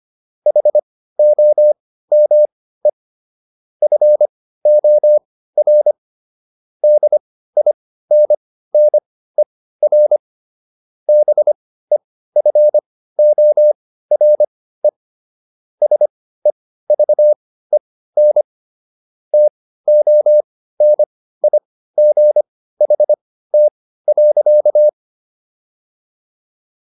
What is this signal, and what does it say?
Morse code, heard as a keyed tone, about 25 wpm, spaced as for 15 wpm, 600 Hz. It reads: HOME FOR DINNER BEFORE SEVEN TONIGHT.